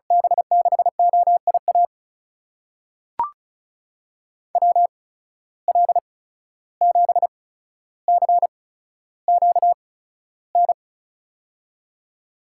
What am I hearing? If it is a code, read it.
Morse code, heard as a keyed tone, about 36 wpm, spaced as for 14 wpm, 700 Hz. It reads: B6OIA WL7CQN
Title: USA Call Signs - 6 Characters - 8x Character Spacing - 35wpm